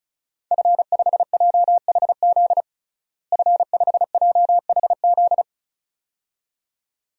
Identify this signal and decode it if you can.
Morse code, heard as a keyed tone, about 36 wpm, 700 Hz. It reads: F5JHZ F5JHZ
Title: DXCC-100 Call Signs - 5 Characters - Repeated 2x - 35wpm